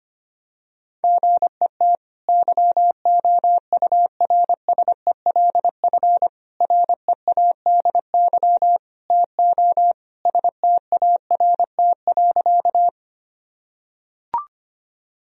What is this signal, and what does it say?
Morse code, heard as a keyed tone, about 25 wpm, 700 Hz. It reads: GET YOURSELF READY TO START.
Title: Sentences from Top 1000 Words - Review Part 1 - 25wpm